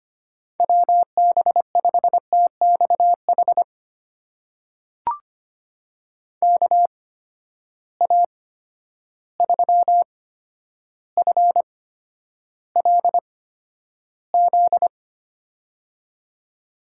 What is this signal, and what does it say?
Morse code, heard as a keyed tone, about 25 wpm, spaced as for 11 wpm, 700 Hz. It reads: WB5TXH KA3FLZ
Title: USA Call Signs - 6 Characters - 8x Character Spacing - 25wpm